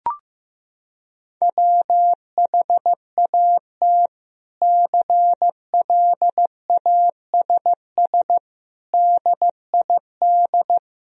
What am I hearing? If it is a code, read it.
Morse code, heard as a keyed tone, about 16 wpm, 700 Hz. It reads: WHAT CLASS DID
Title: Sentences from Top 1000 Words - Review Part 2 - 15wpm